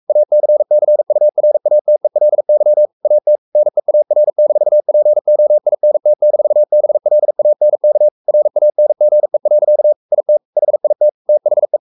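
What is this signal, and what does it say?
Morse code, heard as a keyed tone, about 44 wpm, 600 Hz. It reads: ACCURATELY AT NEAR-POINT-BLANK RANGE. IT HIT THE
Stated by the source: Morse code at 43 wpm sending Ken Follett's novel Never in English, from CW med Gnister.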